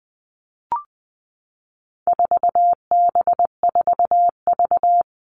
Morse code, heard as a keyed tone, about 20 wpm, 700 Hz. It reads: 4B4V
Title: All Cut Numbers - 4-Digit 20wpm